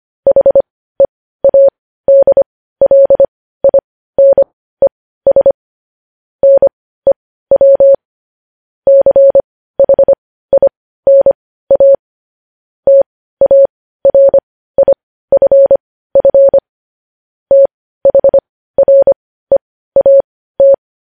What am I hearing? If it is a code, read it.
Morse code, heard as a keyed tone, about 25 wpm, spaced as for 15 wpm, 550 Hz. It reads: HEADLINES NEW CHINA TARIFF THREAT
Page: news-headlines-in-morse-code-at-15-wpm